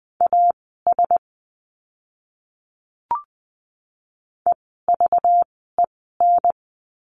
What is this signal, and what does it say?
Morse code, heard as a keyed tone, about 20 wpm, spaced as for 15 wpm, 700 Hz. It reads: AS EVEN
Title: 244 Lesson 39f - Letter z - Words 20wpm